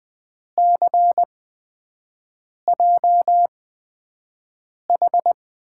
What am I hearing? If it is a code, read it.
Morse code, heard as a keyed tone, about 21 wpm, spaced as for 5 wpm, 700 Hz. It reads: CJH
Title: USA Call Signs - 8x Character Spacing - 20wpm